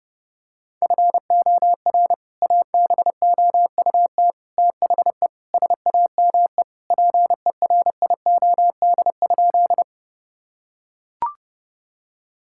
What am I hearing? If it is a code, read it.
Morse code, heard as a keyed tone, about 32 wpm, 700 Hz. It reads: FOR ABOUT THE SAME PERIOD?
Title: Sentences from Top 1000 Words - Review Part 5 - 30wpm